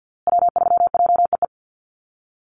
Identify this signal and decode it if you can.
Morse code, heard as a keyed tone, about 50 wpm, 700 Hz. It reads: W3JEE
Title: USA Call Signs Repeated 2x 50wpm